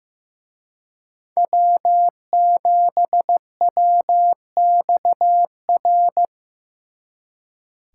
Morse code, heard as a keyed tone, about 18 wpm, 700 Hz. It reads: W7WXR
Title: Speed Racing - USA 1x3 Call Signs 15wpm